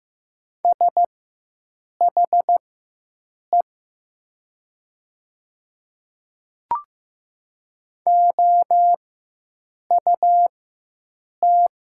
Morse code, heard as a keyed tone, about 15 wpm, spaced as for 7 wpm, 700 Hz. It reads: SHE OUT
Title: 4x Character Spacing - Top 200 Words - 15wpm